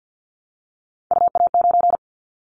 Morse code, heard as a keyed tone, about 50 wpm, 700 Hz. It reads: UA9
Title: Callsign Prefixes for Top 50 DXCC - 50wpm